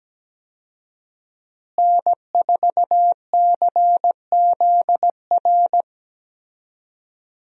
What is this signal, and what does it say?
Morse code, heard as a keyed tone, about 17 wpm, 700 Hz. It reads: N4CZR
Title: USA Call Signs - 5 Characters - Repeated 2x - 17wpm